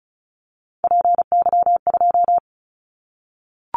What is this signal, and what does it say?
Morse code, heard as a keyed tone, about 34 wpm, 700 Hz. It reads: PY2
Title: Callsign Prefixes for Top 100 DXCC - 35wpm